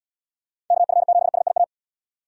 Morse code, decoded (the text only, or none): KF7SIA